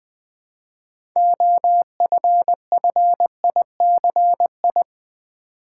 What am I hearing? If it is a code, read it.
Morse code, heard as a keyed tone, about 20 wpm, 700 Hz. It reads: OFFICI
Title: Build A Word - 2 Letters At A Time - 8-Letter Words - 20wpm